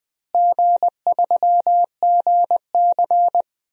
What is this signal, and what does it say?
Morse code, heard as a keyed tone, about 20 wpm, 700 Hz. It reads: G3GC
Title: DXCC-100 Call Signs - 4 Characters - Repeated 2x - 20wpm